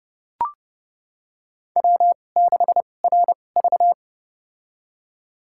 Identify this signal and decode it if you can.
Morse code, heard as a keyed tone, about 32 wpm, spaced as for 22 wpm, 700 Hz. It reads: W6RV
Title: USA Call Signs - 4 Characters - 2x Character Spacing - 30wpm